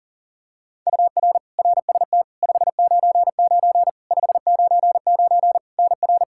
Morse code, heard as a keyed tone, about 40 wpm, 700 Hz. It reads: UR RST 599 599 NR
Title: K1USN SST QSO Practice 40wpm